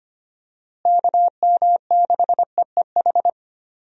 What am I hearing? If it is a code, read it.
Morse code, heard as a keyed tone, about 25 wpm, 700 Hz. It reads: KM6EEH
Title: USA Call Signs - 6 Characters - Repeated 2x - 25wpm